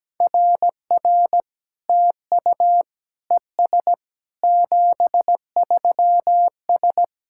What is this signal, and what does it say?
Morse code, heard as a keyed tone, about 17 wpm, 700 Hz. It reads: RR TU ES 73S
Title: SOTA QSO Practice 17wpm